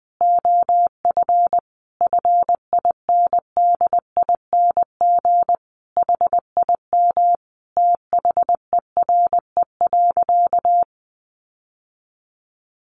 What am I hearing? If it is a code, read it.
Morse code, heard as a keyed tone, about 19 wpm, 700 Hz. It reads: OF FINDING HIM THERE.
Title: Sentences from Top 1100 Words 20wpm